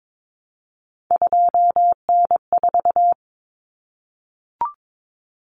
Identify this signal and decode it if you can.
Morse code, heard as a keyed tone, about 22 wpm, 700 Hz. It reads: 2N4
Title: Common Cut Numbers - 3-Digit 22wpm